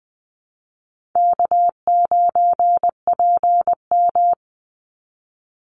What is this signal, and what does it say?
Morse code, heard as a keyed tone, about 20 wpm, 700 Hz. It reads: K9PM
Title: NA SOTA Activator Callsigns 20wpm